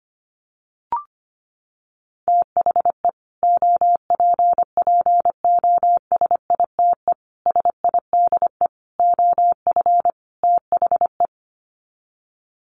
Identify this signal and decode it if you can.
Morse code, heard as a keyed tone, about 25 wpm, 700 Hz. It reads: THE OPPOSITE SIDE OF THE
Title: Sets of 5 Words Encore 25wpm